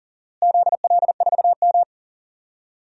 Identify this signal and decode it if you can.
Morse code, heard as a keyed tone, about 40 wpm, 700 Hz. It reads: ZL4M
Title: DXCC-100 Call Signs - 4 Characters - Repeated 2x - 40wpm